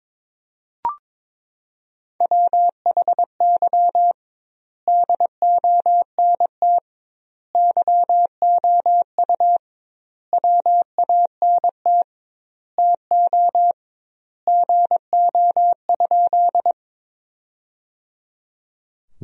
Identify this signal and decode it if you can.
Morse code, heard as a keyed tone, about 21 wpm, 700 Hz. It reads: WHY DONT YOU WANT TO GO?
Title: 2x Word-Spacing - Sentences from Top 300 Words - 22wpm